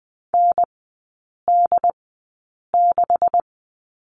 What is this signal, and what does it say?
Morse code, heard as a keyed tone, about 20 wpm, spaced as for 8 wpm, 700 Hz. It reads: ND6
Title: 138 Lesson 23e - Letter f - Three Characters 20wpm